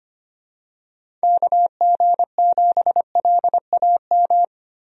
Morse code, heard as a keyed tone, about 25 wpm, 700 Hz. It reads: KG7LAM
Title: USA Call Signs - 6 Characters - Repeated 3x - 25wpm